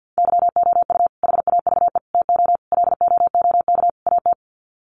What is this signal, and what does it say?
Morse code, heard as a keyed tone, about 50 wpm, 700 Hz. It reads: YOU HAVE TO LOOK AT
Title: Sets of 5 Words Encore 50wpm